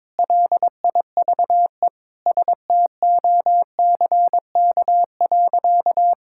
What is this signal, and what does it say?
Morse code, heard as a keyed tone, about 22 wpm, 700 Hz. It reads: LIVESTOCK.
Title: Sentences from Top 1000 Words - Review Part 5 - 22wpm